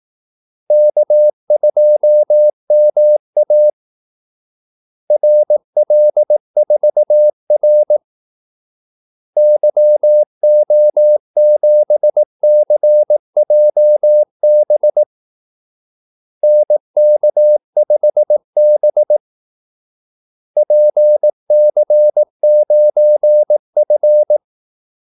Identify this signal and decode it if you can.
Morse code, heard as a keyed tone, about 18 wpm, 600 Hz. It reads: K2MA RL4R YO7CJB NK5B PC9F
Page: Kaldesignaler 18 wpm